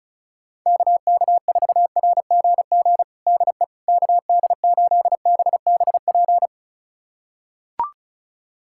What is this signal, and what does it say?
Morse code, heard as a keyed tone, about 36 wpm, 700 Hz. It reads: KK4RGG DE KD8BBP
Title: USA Callsign DE Callsign 35wpm